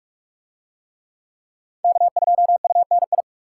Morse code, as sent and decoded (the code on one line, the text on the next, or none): -.- ..--- ..- -. ..
K2UNI